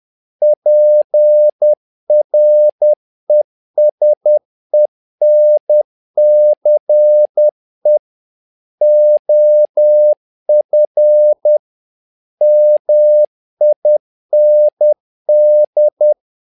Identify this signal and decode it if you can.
Morse code, heard as a keyed tone, about 10 wpm, 600 Hz. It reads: PRESENCE OF MIND
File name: learning center / tower trouble / tower trouble 10wpm